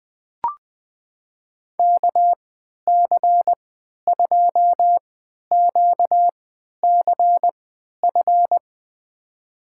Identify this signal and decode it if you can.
Morse code, heard as a keyed tone, about 21 wpm, spaced as for 11 wpm, 700 Hz. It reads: KC2QCF
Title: USA Call Signs - 3x Character Spacing - 20wpm